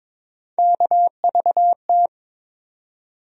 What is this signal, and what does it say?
Morse code, heard as a keyed tone, about 21 wpm, 700 Hz. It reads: KVT
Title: USA Call Sign Postfixes - 3 Letters 22wpm